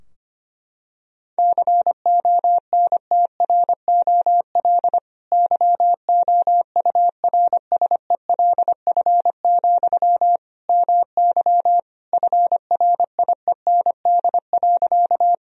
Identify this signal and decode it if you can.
Morse code, heard as a keyed tone, about 25 wpm, 700 Hz. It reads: CONTROL YOURSELF, MY FRIEND.
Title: Sentences from Top 1000 Words - Review Part 3 - 25wpm